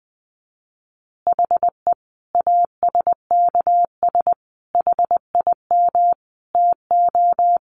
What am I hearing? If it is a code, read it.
Morse code, heard as a keyed tone, about 20 wpm, 700 Hz. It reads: HE ASKS HIM TO
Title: Easy Phrases 20wpm